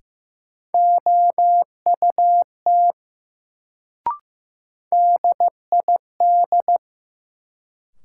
Morse code, heard as a keyed tone, about 15 wpm, 700 Hz. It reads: OUT DID
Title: Sentences from Top 100 Words - New Word Drill - 15wpm